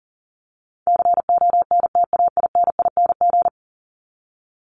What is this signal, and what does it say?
Morse code, heard as a keyed tone, about 40 wpm, 700 Hz. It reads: CONTAINING
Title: ING Suffix - Top 100 Words with Base Words 40wpm